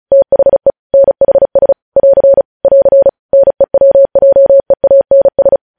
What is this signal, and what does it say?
Morse code, heard as a keyed tone, about 36 wpm, 550 Hz. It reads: THE NHS <AR> <AR> NEWJEANS
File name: podcast / News headlines-1x-35x35